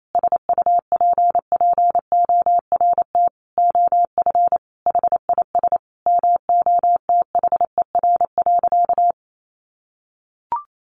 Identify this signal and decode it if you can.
Morse code, heard as a keyed tone, about 28 wpm, 700 Hz. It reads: SUPPORT OF HIS MOTHER.
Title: Sentences from Top 800 Words 28wpm